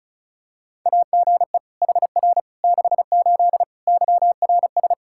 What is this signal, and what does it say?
Morse code, heard as a keyed tone, about 34 wpm, 700 Hz. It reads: AGE HR 68 YRS
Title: K1USN SST QSO Practice 35wpm